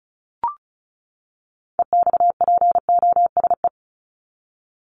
Morse code, heard as a keyed tone, about 34 wpm, 700 Hz. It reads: EXPOSE
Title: Top 1501-1750 Words 35wpm